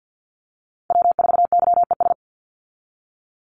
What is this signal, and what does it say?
Morse code, heard as a keyed tone, about 50 wpm, 700 Hz. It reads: W4YES
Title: Call Signs - 50wpm